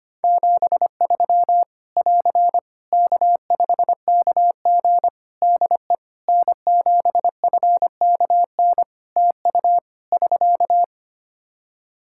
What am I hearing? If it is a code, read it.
Morse code, heard as a keyed tone, about 25 wpm, 700 Hz. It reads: 73 <AR> K5KG DE N7FKN TU <SK>
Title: K1USN SST QSO Practice 25wpm